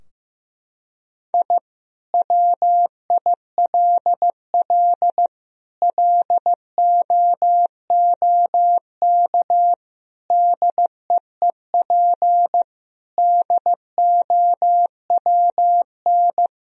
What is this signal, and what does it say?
Morse code, heard as a keyed tone, about 15 wpm, 700 Hz. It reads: I WILL LOOK DEEP DOWN
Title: Sentences from Top 1000 Words - Review Part 3 - 15wpm